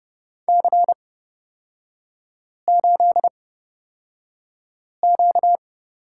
Morse code, heard as a keyed tone, about 30 wpm, spaced as for 5 wpm, 700 Hz. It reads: C8Q
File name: Lesson 38e - Letter q - Three Characters 30wpm